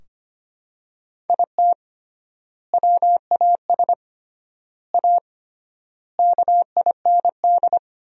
Morse code, heard as a keyed tone, about 25 wpm, 700 Hz. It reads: IT WAS A KIND
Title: 3x Word-Spacing - Sentences from Top 500 Words - 25wpm